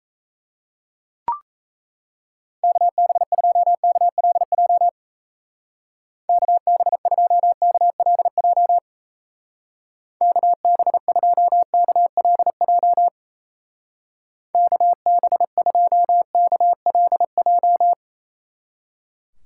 Morse code, heard as a keyed tone, about 35 wpm, 700 Hz. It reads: KB2KLJ KB2KLJ KB2KLJ KB2KLJ
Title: Speed Racing - USA 2x3 Call Signs 28wpm